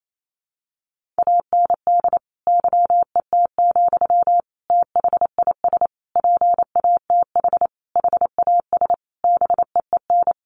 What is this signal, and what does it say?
Morse code, heard as a keyed tone, about 28 wpm, 700 Hz. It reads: AND YET, THIS PATH HAS BEEN
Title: Sentences from Top 1400 Words 28wpm